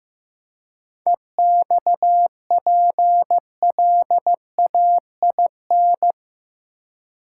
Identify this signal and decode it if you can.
Morse code, heard as a keyed tone, about 15 wpm, 700 Hz. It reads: EXPLAIN
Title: Seven Letter Words - 15wpm